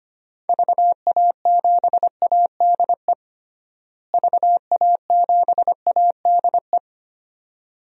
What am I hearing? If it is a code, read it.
Morse code, heard as a keyed tone, about 25 wpm, 700 Hz. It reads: VA7ADE VA7ADE
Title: DXCC-100 Call Signs - 6 Characters - Repeated 2x - 25wpm